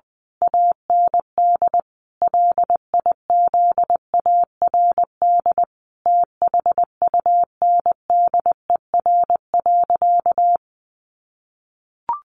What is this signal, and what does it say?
Morse code, heard as a keyed tone, about 20 wpm, 700 Hz. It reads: AND LIZARD THUNDER.
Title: A Sound Of Thunder 20wpm